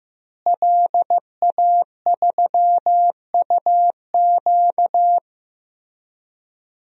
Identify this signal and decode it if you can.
Morse code, heard as a keyed tone, about 16 wpm, 700 Hz. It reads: LA3UQ
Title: DXCC-100 Call Signs - 5 Characters - Repeated 2x - 15wpm